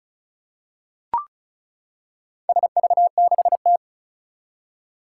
Morse code, heard as a keyed tone, about 36 wpm, 690 Hz. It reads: SV6T
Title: Four Challenging Letter-Number Combinations 35wpm